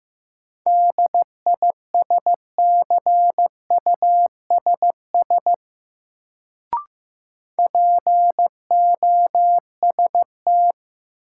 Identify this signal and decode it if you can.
Morse code, heard as a keyed tone, about 15 wpm, 700 Hz. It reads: DISCUSS POST
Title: Sentences from Top 1000 Words - New Word Drill - 15wpm